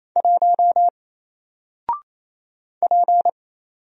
Morse code, heard as a keyed tone, about 27 wpm, 700 Hz. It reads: 1 P
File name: Single-Letter-Number-28wpm